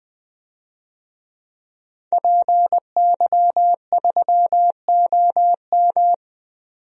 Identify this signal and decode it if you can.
Morse code, heard as a keyed tone, about 20 wpm, 700 Hz. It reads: PY3OM
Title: DXCC-100 Call Signs - 5 Characters - 20wpm